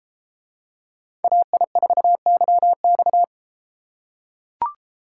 Morse code, heard as a keyed tone, about 32 wpm, 700 Hz. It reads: AI4YX